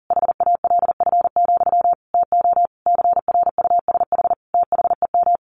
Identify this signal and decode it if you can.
Morse code, heard as a keyed tone, about 40 wpm, 700 Hz. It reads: HALF, TO CRUSH THEM
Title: A Sound Of Thunder 40wpm